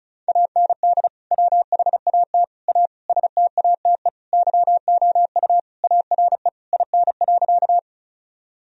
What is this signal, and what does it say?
Morse code, heard as a keyed tone, about 36 wpm, 700 Hz. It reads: AND WHAT A STATE YOU ARE IN.
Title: Sentences from Top 1000 Words - Review Part 2 - 35wpm